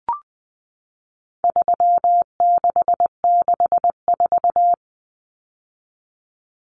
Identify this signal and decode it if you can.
Morse code, heard as a keyed tone, about 20 wpm, 700 Hz. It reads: 3664
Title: Common Cut Numbers - 4-Digit 20wpm